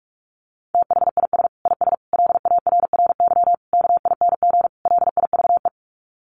Morse code, heard as a keyed tone, about 44 wpm, 700 Hz. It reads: THIS IS LARRY KING LIVE
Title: Sets of 5 Words Encore2 45wpm